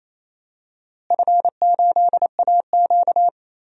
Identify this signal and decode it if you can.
Morse code, heard as a keyed tone, about 28 wpm, 700 Hz. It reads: F8AQ
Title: DXCC-100 Call Signs - 4 Characters - Repeated 2x - 28wpm